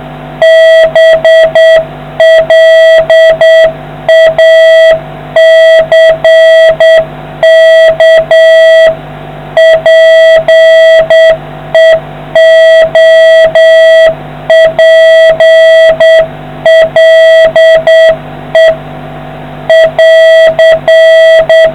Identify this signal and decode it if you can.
Morse code, heard as a keyed tone, about 8 wpm, 650 Hz, with noise. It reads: BLACK PEOPLE <AR>